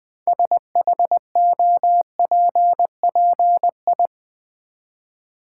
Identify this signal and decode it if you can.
Morse code, heard as a keyed tone, about 20 wpm, 700 Hz. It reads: SHOPPI
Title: Build A Word - 3 Letters At A Time - 8-Letter Words - 20wpm